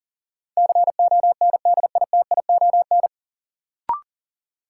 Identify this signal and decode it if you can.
Morse code, heard as a keyed tone, about 40 wpm, 700 Hz. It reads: CONDITION
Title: Sentences from Top 1000 Words - New Word Drill - 40wpm